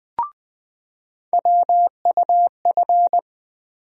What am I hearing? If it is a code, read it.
Morse code, heard as a keyed tone, about 21 wpm, 700 Hz. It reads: WUF
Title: USA Call Sign Postfixes - 3 Letters 20wpm